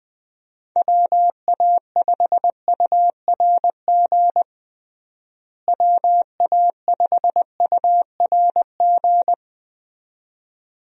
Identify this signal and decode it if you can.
Morse code, heard as a keyed tone, about 20 wpm, 700 Hz. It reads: WA5URG WA5URG
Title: USA Call Signs - 6 Characters - Repeated 2x - 20wpm